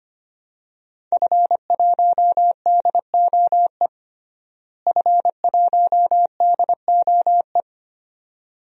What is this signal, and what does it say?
Morse code, heard as a keyed tone, about 25 wpm, 700 Hz. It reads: F1DOE F1DOE
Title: DXCC-100 Call Signs - 5 Characters - Repeated 2x - 25wpm